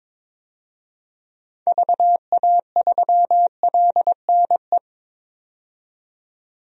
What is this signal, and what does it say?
Morse code, heard as a keyed tone, about 22 wpm, 700 Hz. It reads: VA3LNE
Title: DXCC-100 Call Signs - 6 Characters - 22wpm